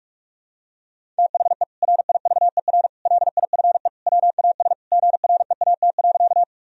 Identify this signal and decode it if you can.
Morse code, heard as a keyed tone, about 42 wpm, 700 Hz. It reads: THE RIVER LIFE WAS GREAT.
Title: Sentences from Top 1000 Words - Review Part 1 - 45wpm